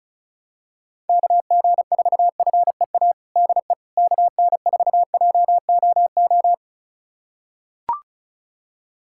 Morse code, heard as a keyed tone, about 36 wpm, 700 Hz. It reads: KG4FEA DE KN4JOO
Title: USA Callsign DE Callsign - 6 Characters - 35wpm